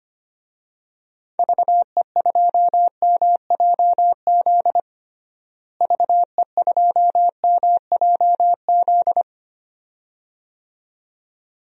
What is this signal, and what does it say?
Morse code, heard as a keyed tone, about 25 wpm, 700 Hz. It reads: VE2MJZ VE2MJZ
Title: DXCC-100 Call Signs - 6 Characters - Repeated 2x - 25wpm